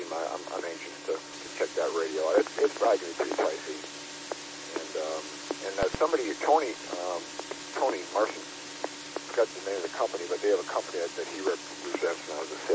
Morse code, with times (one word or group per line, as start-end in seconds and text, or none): none